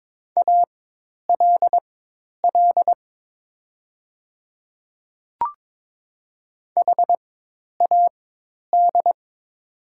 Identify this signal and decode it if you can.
Morse code, heard as a keyed tone, about 22 wpm, spaced as for 10 wpm, 700 Hz. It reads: ALL HAD
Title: 4x Character Spacing - Top 100 Words - 22wpm